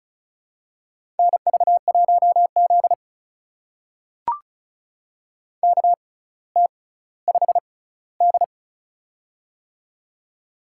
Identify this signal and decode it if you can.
Morse code, heard as a keyed tone, about 36 wpm, spaced as for 18 wpm, 700 Hz. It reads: NV1Z KT5D
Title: USA Call Signs - 4 Characters - 6x Character Spacing - 35wpm